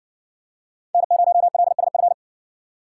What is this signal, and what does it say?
Morse code, heard as a keyed tone, about 60 wpm, 700 Hz. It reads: N0LSF